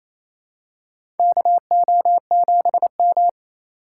Morse code, heard as a keyed tone, about 28 wpm, 700 Hz. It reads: KO7M
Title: USA Call Signs - 4 Characters - 2x Character Spacing - 28wpm